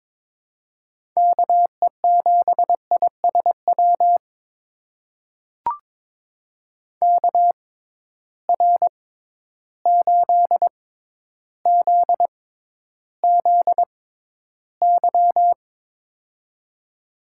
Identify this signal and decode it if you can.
Morse code, heard as a keyed tone, about 22 wpm, spaced as for 11 wpm, 700 Hz. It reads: KE7ISW KR8ZZY
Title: USA Call Signs - 6 Characters - 6x Character Spacing - 22wpm